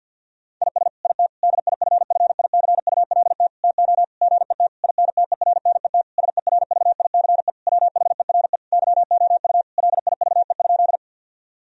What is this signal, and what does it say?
Morse code, heard as a keyed tone, about 50 wpm, 700 Hz. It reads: IS IT DIFFICULT TO GET INTERNET SERVICE WHERE YOU LIVE?
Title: Sentences from Top 1000 Words - Review Part 4 - 50wpm